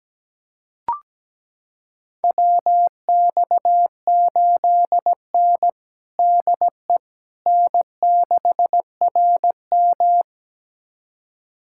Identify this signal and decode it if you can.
Morse code, heard as a keyed tone, about 17 wpm, 700 Hz. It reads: WX8N DE N6RM
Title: USA Callsign DE Callsign - 4 Characters - 17wpm